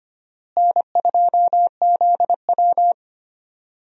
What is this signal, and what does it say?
Morse code, heard as a keyed tone, about 25 wpm, 700 Hz. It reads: N2ZW
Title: USA Call Signs - 4 Characters - Repeated 2x - 25wpm